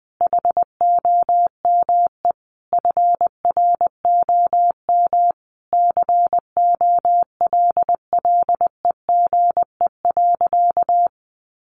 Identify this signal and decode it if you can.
Morse code, heard as a keyed tone, about 20 wpm, 700 Hz. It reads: HOME FROM COLLEGE.
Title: Sentences from Top 1300 Words 20wpm